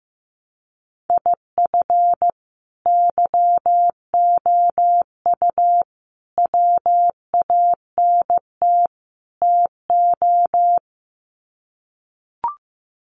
Morse code, heard as a keyed tone, about 15 wpm, 700 Hz. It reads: IF YOU WANT TO
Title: Sets of 4 Words 15wpm